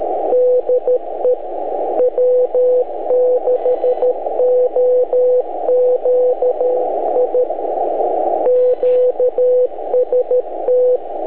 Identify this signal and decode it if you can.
Morse code, heard as a keyed tone, about 13 wpm, 500 Hz, with noise. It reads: DE W6OQI QST
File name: w6oqi-shack-jan22-snip1